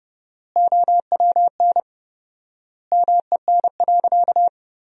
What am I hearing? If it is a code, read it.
Morse code, heard as a keyed tone, about 30 wpm, 700 Hz. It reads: OWN MEN.
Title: 4x Word-Spacing - Sentences from Top 300 Words - 30wpm